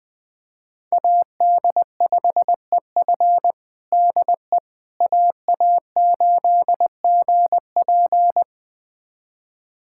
Morse code, heard as a keyed tone, about 20 wpm, 700 Hz. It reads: AD5EF DE AA8GP
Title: USA Callsign DE Callsign - 5 Characters - 20wpm